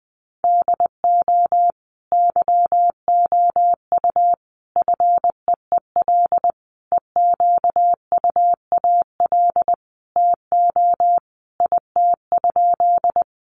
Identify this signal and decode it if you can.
Morse code, heard as a keyed tone, about 20 wpm, 700 Hz. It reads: DO YOU FEEL EQUAL TO IT?
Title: Sentences from Top 1400 Words 20wpm